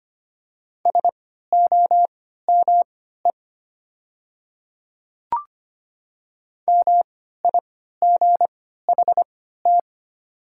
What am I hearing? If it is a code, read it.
Morse code, heard as a keyed tone, about 25 wpm, spaced as for 14 wpm, 690 Hz. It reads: SOME MIGHT
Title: 3x Character Spacing - Top 300 Words - 25wpm